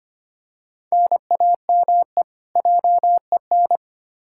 Morse code, heard as a keyed tone, about 25 wpm, 700 Hz. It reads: NAME JEN
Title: K1USN SST QSO Practice 25wpm